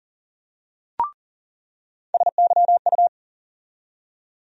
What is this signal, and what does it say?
Morse code, heard as a keyed tone, about 40 wpm, 700 Hz. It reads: SYU
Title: USA Call Sign Postfixes - 3 Letters 40wpm